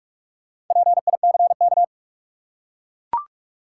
Morse code, heard as a keyed tone, about 42 wpm, 700 Hz. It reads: PICK
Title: Four Letter Words - 45wpm